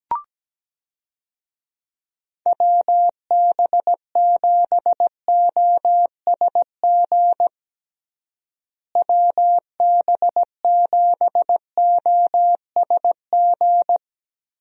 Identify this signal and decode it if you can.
Morse code, heard as a keyed tone, about 17 wpm, 700 Hz. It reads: WB7OSG WB7OSG
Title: USA Call Signs - 6 Characters - Repeated 3x - 17wpm